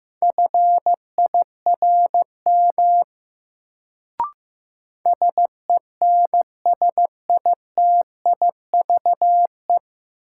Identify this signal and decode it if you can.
Morse code, heard as a keyed tone, about 16 wpm, 700 Hz. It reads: FIRM SENSITIVE
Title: Sentences from Top 1400 Words - New Word Drill - 15wpm